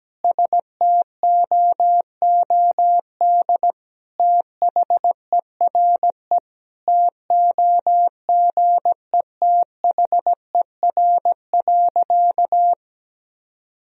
Morse code, heard as a keyed tone, about 17 wpm, 700 Hz. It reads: STOOD THERE TOGETHER.
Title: Sentences from Top 1000 Words - Review Part 2 - 17wpm